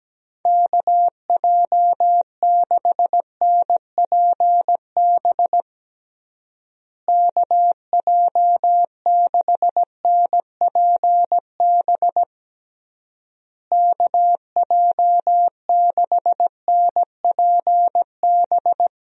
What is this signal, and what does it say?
Morse code, heard as a keyed tone, about 17 wpm, 700 Hz. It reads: KJ6NPB KJ6NPB KJ6NPB
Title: USA Call Signs - 6 Characters - Repeated 3x - 17wpm